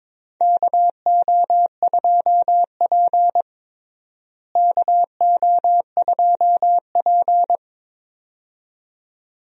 Morse code, heard as a keyed tone, about 22 wpm, 700 Hz. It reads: KO2P KO2P
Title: USA Call Signs - 4 Characters - Repeated 2x - 22wpm